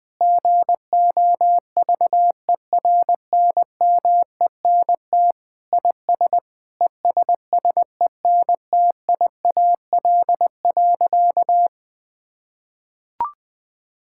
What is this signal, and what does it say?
Morse code, heard as a keyed tone, about 20 wpm, 700 Hz. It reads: GOVERNMENT IS ESSENTIAL.
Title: Sentences with 7-Word Phrases - 20wpm